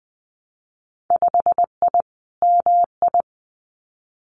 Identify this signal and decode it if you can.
Morse code, heard as a keyed tone, about 20 wpm, 700 Hz. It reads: 5I MI
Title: Winter Field Day Exchanges Encore 20wpm